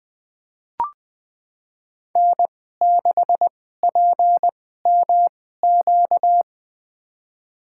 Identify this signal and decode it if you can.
Morse code, heard as a keyed tone, about 20 wpm, spaced as for 14 wpm, 700 Hz. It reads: N6PMQ
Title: USA Call Signs - 5 Characters - 2x Character Spacing - 20wpm